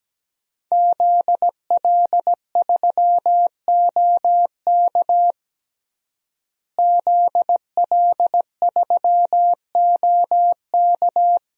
Morse code, heard as a keyed tone, about 17 wpm, 700 Hz. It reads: ZL3OK ZL3OK
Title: DXCC-100 Call Signs - 5 Characters - Repeated 2x - 17wpm